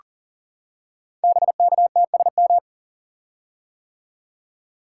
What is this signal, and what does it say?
Morse code, heard as a keyed tone, about 39 wpm, 690 Hz. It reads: BKTSM
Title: Code Groups - 5 Letters - 40wpm